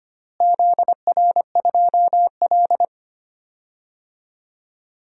Morse code, heard as a keyed tone, about 25 wpm, 700 Hz. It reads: ZR2L
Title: DXCC-100 Call Signs - 4 Characters - Repeated 2x - 25wpm